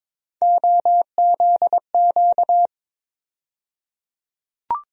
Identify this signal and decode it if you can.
Morse code, heard as a keyed tone, about 22 wpm, 700 Hz. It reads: OZQ
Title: USA Call Sign Postfixes - 3 Letters 22wpm